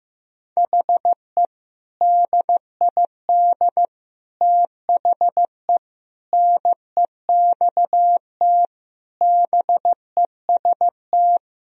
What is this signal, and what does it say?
Morse code, heard as a keyed tone, about 15 wpm, 700 Hz. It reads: HE DID THE NEXT BEST
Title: Sentences from Top 1000 Words - Review Part 1 - 15wpm